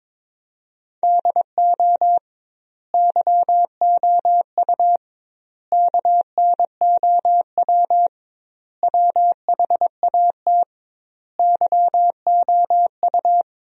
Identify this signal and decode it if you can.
Morse code, heard as a keyed tone, about 22 wpm, 700 Hz. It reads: DO YOU KNOW WHAT YOU
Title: 2x Word-Spacing - Sentences from Top 100 Words - 22wpm